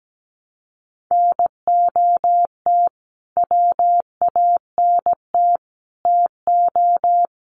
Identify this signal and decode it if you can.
Morse code, heard as a keyed tone, about 17 wpm, 700 Hz. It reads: NOT WANT TO
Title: Sets of 3 Words 17wpm